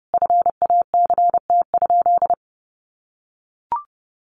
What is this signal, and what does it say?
Morse code, heard as a keyed tone, about 30 wpm, 700 Hz. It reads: FACT?
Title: Sentences from Top 600 Words 30wpm